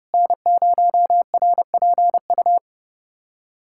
Morse code, heard as a keyed tone, about 30 wpm, 700 Hz. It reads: N0RPU
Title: USA Callsign DE Callsign 30wpm